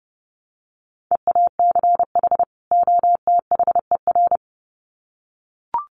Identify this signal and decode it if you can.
Morse code, heard as a keyed tone, about 30 wpm, 700 Hz. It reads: EACH OTHER
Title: Sets of 2 Words 30wpm